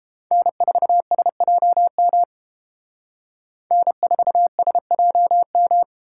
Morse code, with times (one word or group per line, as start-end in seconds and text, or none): 0.3-2.2 N4SJM
3.7-5.8 N4SJM